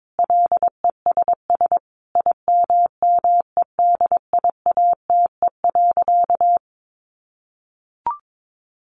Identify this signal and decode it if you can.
Morse code, heard as a keyed tone, about 22 wpm, 700 Hz. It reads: LESS IMMEDIATE.
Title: Sentences from Top 1300 Words 22wpm